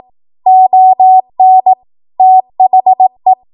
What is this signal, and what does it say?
Morse code, heard as a keyed tone, about 18 wpm, 750 Hz. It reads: ON THE
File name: Morse / 18WPM